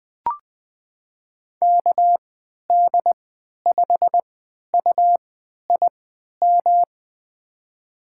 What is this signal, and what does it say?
Morse code, heard as a keyed tone, about 21 wpm, spaced as for 11 wpm, 700 Hz. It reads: KD5UIM
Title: USA Call Signs - 3x Character Spacing - 20wpm